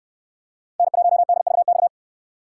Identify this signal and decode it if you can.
Morse code, heard as a keyed tone, about 67 wpm, 700 Hz. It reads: N1DVX